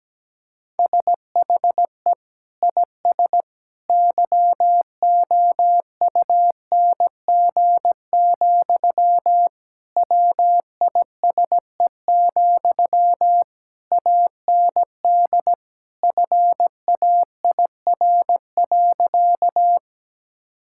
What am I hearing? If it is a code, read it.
Morse code, heard as a keyed tone, about 17 wpm, 700 Hz. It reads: SHE IS YOUNG, WISE, AND FAIR.
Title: Sentences from Top 1600 Words 17wpm